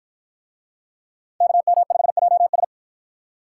Morse code, heard as a keyed tone, about 48 wpm, 700 Hz. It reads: KM5JS